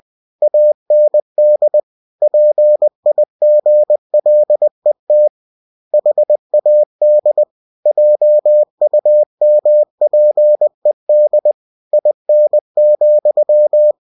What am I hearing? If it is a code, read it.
Morse code, heard as a keyed tone, about 20 wpm, 600 Hz. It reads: AND PIGLET HAD JUMPED IN,
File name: cw-books / winnie-the-pooh / winnie-20wpm-page-0063